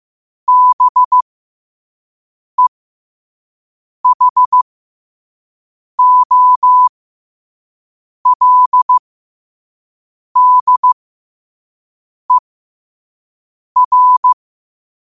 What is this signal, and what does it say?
Morse code, heard as a keyed tone, about 15 wpm, spaced as for 5 wpm, 1000 Hz. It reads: BEHOLDER
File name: qotdfiles / 5wpm